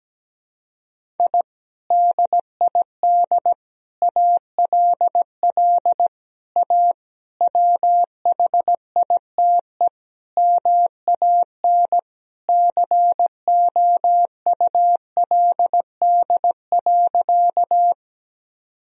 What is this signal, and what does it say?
Morse code, heard as a keyed tone, about 17 wpm, 700 Hz. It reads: I DID ALL A WHITE MAN COULD.
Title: Sentences from Top 1000 Words - Review Part 1 - 17wpm